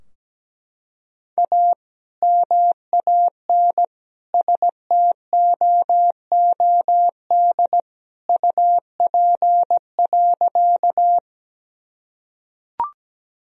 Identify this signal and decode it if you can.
Morse code, heard as a keyed tone, about 17 wpm, 700 Hz. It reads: A MAN STOOD UP.
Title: Sentences from Top 1000 Words - Review Part 2 - 17wpm